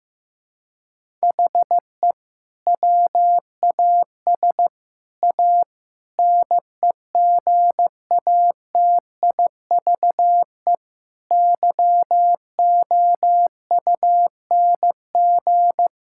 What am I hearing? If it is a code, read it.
Morse code, heard as a keyed tone, about 15 wpm, 700 Hz. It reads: HE WAS A NEGATIVE YOUNG
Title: Sentences from Top 1000 Words - Review Part 6 - 15wpm